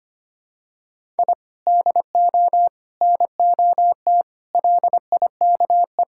Morse code, heard as a keyed tone, about 25 wpm, 700 Hz. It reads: I DO NOT LIKE
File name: practice / 25wpm / Sentences with 4-Word Phrases - 25wpm